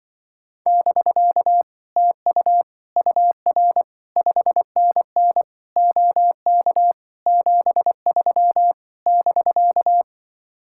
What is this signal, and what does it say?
Morse code, heard as a keyed tone, about 25 wpm, 700 Hz. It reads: <BK> TU UR 5NN OK 73 <BK>
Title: Speed-Racing Common QSO Phrases - 20wpm